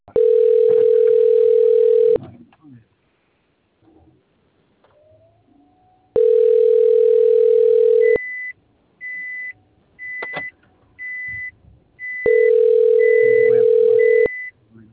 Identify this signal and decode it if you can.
Ringback tone (North America), 440 + 480 Hz, with speech.
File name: out- - - - - .63077